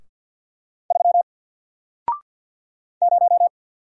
Morse code, heard as a keyed tone, about 48 wpm, 700 Hz. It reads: 3 0
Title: Single Letter-Number Alt 50wpm